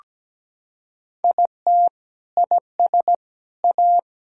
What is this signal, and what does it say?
Morse code, heard as a keyed tone, about 17 wpm, 700 Hz. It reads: IT IS A
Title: Sentences from Top 1000 Words - Review Part 5 - 17wpm